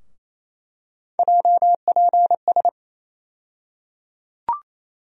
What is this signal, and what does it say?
Morse code, heard as a keyed tone, about 28 wpm, 700 Hz. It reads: JPS
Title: USA Call Sign Postfixes - 3 Letters 28wpm